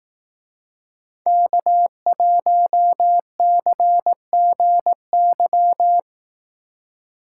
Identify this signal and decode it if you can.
Morse code, heard as a keyed tone, about 18 wpm, 700 Hz. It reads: K1CGY
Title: Speed Racing - USA 1x3 Call Signs 15wpm